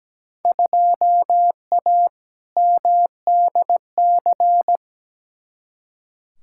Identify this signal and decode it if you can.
Morse code, heard as a keyed tone, about 17 wpm, 700 Hz. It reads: 2A MDC
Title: Field Day Exchanges - 17wpm